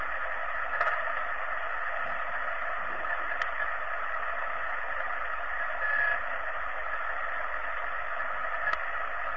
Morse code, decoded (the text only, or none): none